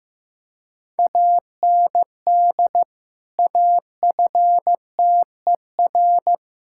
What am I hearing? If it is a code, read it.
Morse code, heard as a keyed tone, about 15 wpm, 700 Hz. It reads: AND AFTER
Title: Sentences from Top 1600 Words 15wpm